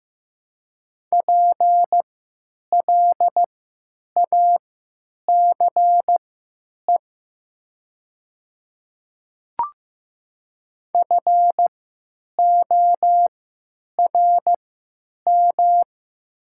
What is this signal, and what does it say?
Morse code, heard as a keyed tone, about 15 wpm, spaced as for 9 wpm, 700 Hz. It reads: PLACE FORM
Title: 3x Character Spacing - Top 200 Words - 15wpm